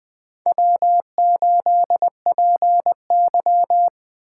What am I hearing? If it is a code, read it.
Morse code, heard as a keyed tone, about 20 wpm, 700 Hz. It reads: W8PY